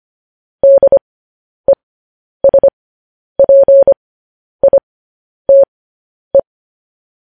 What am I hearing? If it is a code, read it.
Morse code, heard as a keyed tone, about 25 wpm, spaced as for 10 wpm, 550 Hz. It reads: DESPITE